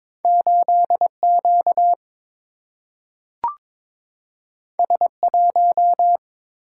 Morse code, heard as a keyed tone, about 22 wpm, 700 Hz. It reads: 8Q S1
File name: practice / 22wpm / Two Letter-Number Combinations - 22wpm